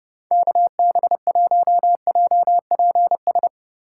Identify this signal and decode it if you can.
Morse code, heard as a keyed tone, about 31 wpm, 700 Hz. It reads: KB1JPS